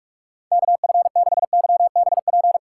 Morse code, heard as a keyed tone, about 47 wpm, 700 Hz. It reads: KF6YBP
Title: USA Call Signs - 2x Character Spacing - 45wpm